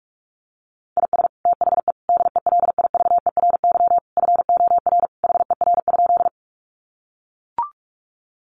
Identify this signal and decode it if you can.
Morse code, heard as a keyed tone, about 44 wpm, 700 Hz. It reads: IS THE DELIVERY FOR HER?
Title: Sentences from Top 1500 Words 45wpm